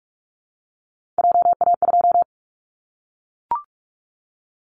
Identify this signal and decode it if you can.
Morse code, heard as a keyed tone, about 44 wpm, 700 Hz. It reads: JA2
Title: Callsign Prefixes for Top 25 DXCC - Repeated 3x - 45wpm